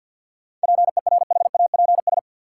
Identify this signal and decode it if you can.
Morse code, heard as a keyed tone, about 53 wpm, 700 Hz. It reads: PERHAPS
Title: Sentences from Top 500 Words - New Word Drill - 50wpm